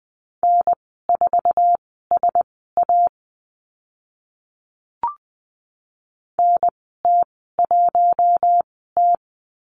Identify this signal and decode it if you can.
Morse code, heard as a keyed tone, about 20 wpm, spaced as for 14 wpm, 700 Hz. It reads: N4SA NT1T
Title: 048 Lesson 9g - Number 4 - Call Signs 20wpm